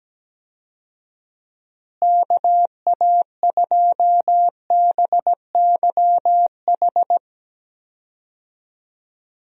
Morse code, heard as a keyed tone, about 17 wpm, 700 Hz. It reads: KA2BYH
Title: USA Call Signs - 6 Characters - Repeated 2x - 17wpm